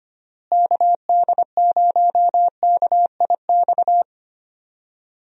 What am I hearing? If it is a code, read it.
Morse code, heard as a keyed tone, about 25 wpm, 700 Hz. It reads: KD0KIX
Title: USA Call Signs - 6 Characters - Repeated 3x - 25wpm